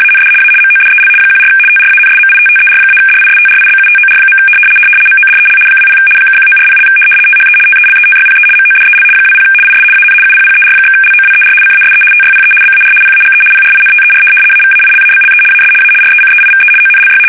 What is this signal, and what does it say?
RTTY, about 75 baud. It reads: AND A MODEL 14 TRANSMITTER-DISTRIBUTOR. MODEL 20 (1940) THE TELETYPE MODEL 20 IS AN UPPER/LOWER CASE TYPE BAR PAGE PRINTER AVAILABLE AS A RECEIVE ONLY
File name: fsk850-75B / teletype-corp